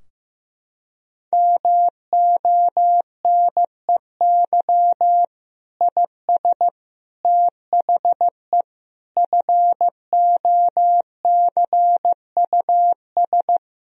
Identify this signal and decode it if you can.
Morse code, heard as a keyed tone, about 16 wpm, 700 Hz. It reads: MONEY IS THE FOCUS
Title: Sentences from Top 1000 Words - Review Part 4 - 15wpm